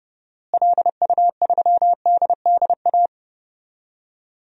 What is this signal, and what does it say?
Morse code, heard as a keyed tone, about 30 wpm, 700 Hz. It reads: LU3DDA
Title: DXCC-100 Call Signs - 6 Characters - Repeated 2x - 30wpm